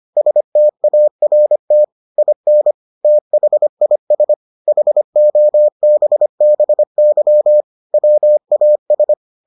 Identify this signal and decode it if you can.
Morse code, heard as a keyed tone, about 25 wpm, 600 Hz. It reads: START IN THIS HOBBY WAS